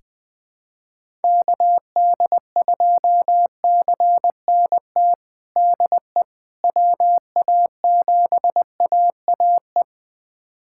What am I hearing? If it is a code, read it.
Morse code, heard as a keyed tone, about 21 wpm, 700 Hz. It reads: KD2CNT DE WA7AAE
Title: USA Callsign DE Callsign - 6 Characters - 20wpm